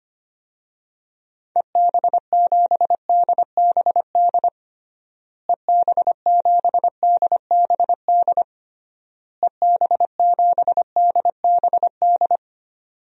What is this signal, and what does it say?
Morse code, heard as a keyed tone, about 25 wpm, 700 Hz. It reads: EB7DBD EB7DBD EB7DBD
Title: DXCC-100 Call Signs - 6 Characters - Repeated 3x - 25wpm